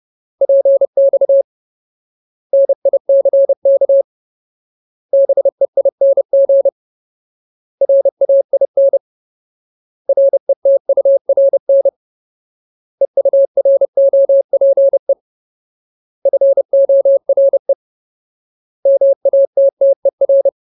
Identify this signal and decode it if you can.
Morse code, heard as a keyed tone, about 30 wpm, 550 Hz. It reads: PX NICK BEING RAIN RETURN EUROPE FORE MATTER
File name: Ultimate Word List / 30WPM Ultimate